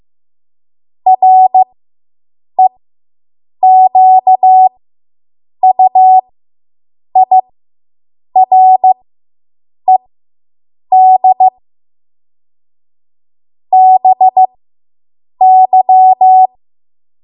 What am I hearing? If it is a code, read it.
Morse code, heard as a keyed tone, about 15 wpm, spaced as for 7 wpm, 750 Hz. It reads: REQUIRED BY